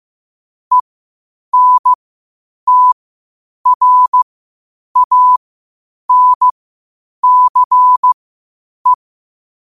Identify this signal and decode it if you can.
Morse code, heard as a keyed tone, about 15 wpm, spaced as for 8 wpm, 1000 Hz. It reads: ENTRANCE
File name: qotdfiles / 8wpm